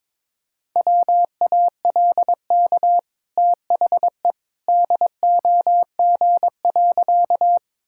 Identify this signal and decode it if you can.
Morse code, heard as a keyed tone, about 21 wpm, 690 Hz. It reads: WALK THE DOG.
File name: practice / 22wpm / Easy Sentences with Hints - 22wpm